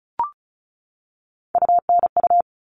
Morse code, heard as a keyed tone, about 34 wpm, 700 Hz. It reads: UNU
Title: All Cut Numbers - 3-Digit 35wpm